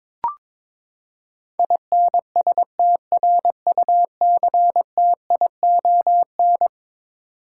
Morse code, heard as a keyed tone, about 22 wpm, 700 Hz. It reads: INSTRUCTION
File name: practice / 22wpm / Eleven Letter Words - 22wpm